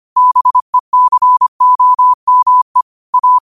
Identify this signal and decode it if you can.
Morse code, heard as a keyed tone, about 25 wpm, 1000 Hz. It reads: BECOME A
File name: qotdfiles / 25wpm